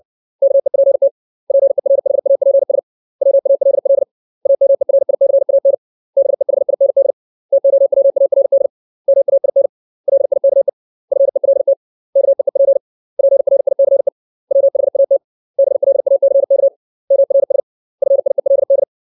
Morse code, heard as a keyed tone, about 60 wpm, 550 Hz. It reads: KEPT PERHAPS WALL AMERICAN BEHIND TOWARD MAIN LIKE FELT KEEP PRICE WHAT BLACK MRS FIELD
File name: 632 words / 632 Words 60WPM